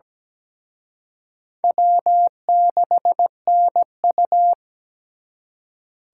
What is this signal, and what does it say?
Morse code, heard as a keyed tone, about 20 wpm, 700 Hz. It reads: W6NU
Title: Speed Racing - USA 1x2 Call Signs 17wpm